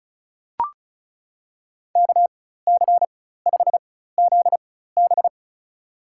Morse code, heard as a keyed tone, about 36 wpm, spaced as for 16 wpm, 700 Hz. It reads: KC5ZB
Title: USA Call Signs - 4x Character Spacing - 35wpm